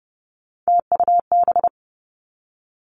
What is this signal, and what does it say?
Morse code, heard as a keyed tone, about 30 wpm, 700 Hz. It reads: TUB
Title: Three Challenging Letter-Number Combinations 30wpm